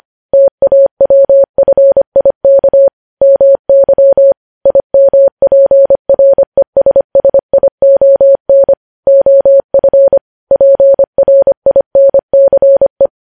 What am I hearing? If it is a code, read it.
Morse code, heard as a keyed tone, about 25 wpm, 550 Hz. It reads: TAWFIK MY IMPRESSION OF PRINCE
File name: podcast / News headlines-1x-25x25